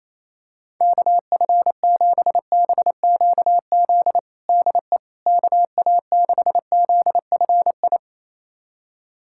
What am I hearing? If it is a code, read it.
Morse code, heard as a keyed tone, about 28 wpm, 700 Hz. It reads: KF7BQZ DE KA6ZFI
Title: USA Callsign DE Callsign - 6 Characters - 28wpm